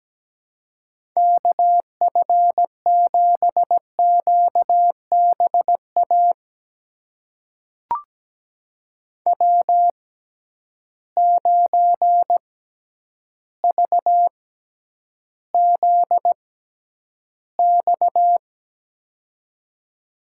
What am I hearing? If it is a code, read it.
Morse code, heard as a keyed tone, about 17 wpm, spaced as for 9 wpm, 700 Hz. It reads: KF7QBA W9VZX
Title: USA Call Signs - 6x Character Spacing - 17wpm